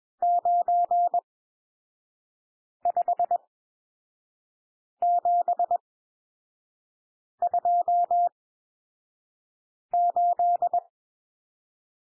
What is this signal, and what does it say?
Morse code, heard as a keyed tone, about 21 wpm, spaced as for 5 wpm, 700 Hz. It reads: 95728
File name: cwmp3 / N1 SOUND NUMBERS 21 wpm